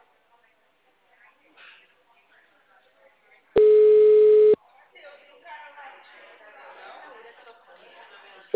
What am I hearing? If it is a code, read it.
Ringback tone (Europe, 425 Hz), with speech.
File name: out- -2011- - - .4738